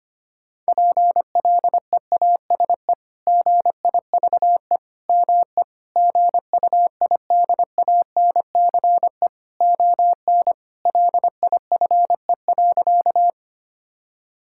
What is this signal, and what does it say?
Morse code, heard as a keyed tone, about 25 wpm, 700 Hz. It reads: PLEASE GIVE ME GUIDANCE ON LIFE.
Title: Sentences from Top 1600 Words 25wpm